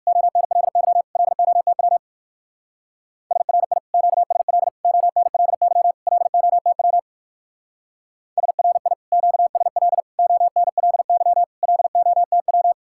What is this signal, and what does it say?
Morse code, heard as a keyed tone, about 55 wpm, 700 Hz. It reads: ONLY LOTW SRI QSL ONLY LOTW SRI QSL ONLY LOTW
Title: Speed-Racing Common QSO Phrases Encore - 45wpm